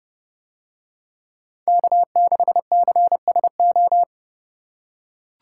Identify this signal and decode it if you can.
Morse code, heard as a keyed tone, about 27 wpm, 700 Hz. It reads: K6CSO